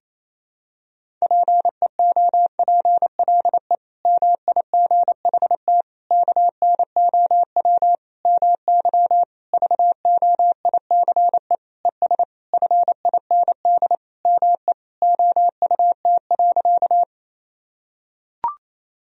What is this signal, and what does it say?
Morse code, heard as a keyed tone, about 28 wpm, 700 Hz. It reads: PEOPLE MIGHT KNOW MY VOICE ES FIND ME OUT.
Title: Sentences from Top 500 Words and Abbreviations 28wpm